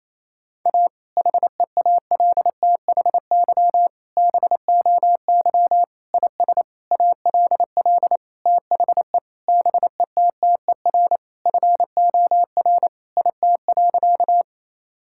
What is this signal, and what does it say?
Morse code, heard as a keyed tone, about 27 wpm, 700 Hz. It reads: A HEALTHY BOY IS ALL THE BETTER FOR IT.
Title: Sentences from Top 1000 Words - Review Part 5 - 28wpm